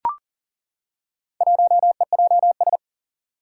Morse code, heard as a keyed tone, about 40 wpm, 700 Hz. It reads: 1EJS
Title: Four Challenging Letter-Number Combinations 40wpm